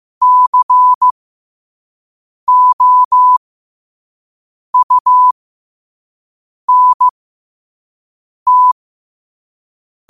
Morse code, heard as a keyed tone, about 15 wpm, spaced as for 5 wpm, 1000 Hz. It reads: COUNT